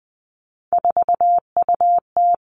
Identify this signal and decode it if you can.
Morse code, heard as a keyed tone, about 20 wpm, 700 Hz. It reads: 4UT
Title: Three Challenging Letter-Number Combinations 20wpm